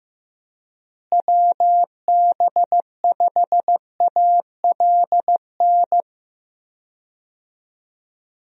Morse code, heard as a keyed tone, about 15 wpm, 700 Hz. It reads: WB5ALN
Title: USA Call Signs - 6 Characters - Repeated 2x - 15wpm